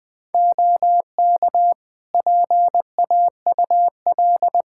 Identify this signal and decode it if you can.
Morse code, heard as a keyed tone, about 20 wpm, 700 Hz. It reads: OK PAUL
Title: K1USN SST QSO Practice 20wpm